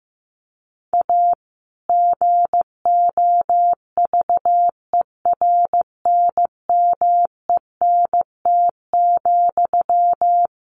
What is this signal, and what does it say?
Morse code, heard as a keyed tone, about 14 wpm, 700 Hz. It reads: A GOVERNMENT,
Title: Sentences from Top 1400 Words 15wpm